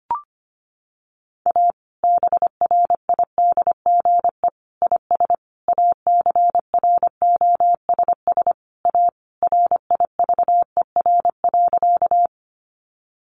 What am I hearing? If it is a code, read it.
Morse code, heard as a keyed tone, about 25 wpm, 700 Hz. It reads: A BRIDGE IS ACROSS A RIVER.
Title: Sentences from Top 1200 Words 25wpm